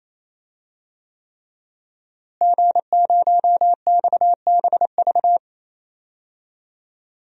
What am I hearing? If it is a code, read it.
Morse code, heard as a keyed tone, about 28 wpm, 700 Hz. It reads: G0XBV
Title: DXCC-100 Call Signs - 5 Characters - Repeated 3x - 28wpm